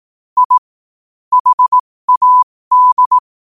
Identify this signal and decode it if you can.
Morse code, heard as a keyed tone, about 18 wpm, spaced as for 15 wpm, 1000 Hz. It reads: I HAD
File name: qotdfiles / 15wpm